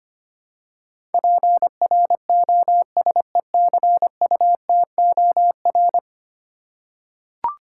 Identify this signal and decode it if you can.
Morse code, heard as a keyed tone, about 25 wpm, 700 Hz. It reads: PROSECUTOR
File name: 10 Letter Words - First 3 Letters Given - 25wpm